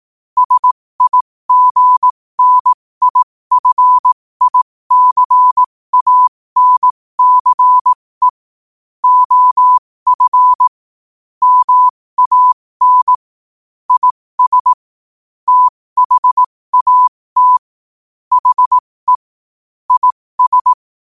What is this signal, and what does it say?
Morse code, heard as a keyed tone, about 18 wpm, spaced as for 15 wpm, 1000 Hz. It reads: SIGNIFICANCE OF MAN IS THAT HE IS
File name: qotdfiles / 15wpm